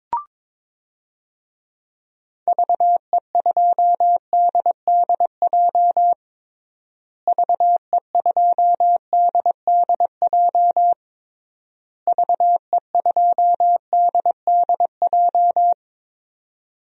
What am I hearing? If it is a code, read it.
Morse code, heard as a keyed tone, about 22 wpm, 700 Hz. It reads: VE2DDJ VE2DDJ VE2DDJ
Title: DXCC-100 Call Signs - 6 Characters - Repeated 3x - 22wpm